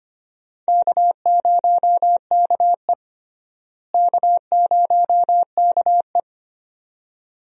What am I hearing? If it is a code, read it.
Morse code, heard as a keyed tone, about 25 wpm, 700 Hz. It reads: K0KE K0KE
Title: USA Call Signs - 4 Characters - Repeated 2x - 25wpm